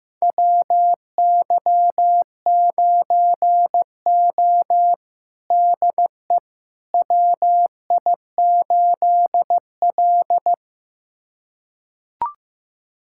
Morse code, heard as a keyed tone, about 15 wpm, 700 Hz. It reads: WY9O DE WI8L
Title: USA Callsign DE Callsign - 4 Characters - 15wpm